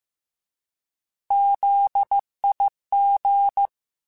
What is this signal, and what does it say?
Morse code, heard as a keyed tone, about 15 wpm, 800 Hz. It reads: ZIG
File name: Alphabetical / Alphabetical mp3s / Zs @13